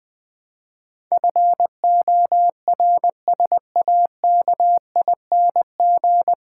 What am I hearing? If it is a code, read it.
Morse code, heard as a keyed tone, about 20 wpm, 700 Hz. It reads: FORSAKING
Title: Pangram Challenge v2 - 20wpm